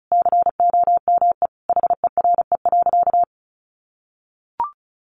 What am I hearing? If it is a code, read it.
Morse code, heard as a keyed tone, about 34 wpm, 700 Hz. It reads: COME HERE.
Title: Sentences from Top 1300 Words 35wpm